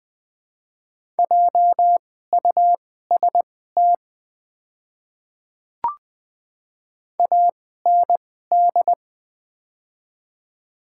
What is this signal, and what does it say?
Morse code, heard as a keyed tone, about 20 wpm, spaced as for 15 wpm, 700 Hz. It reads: JUST AND
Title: 2x Character Spacing - Top 200 Words - 20wpm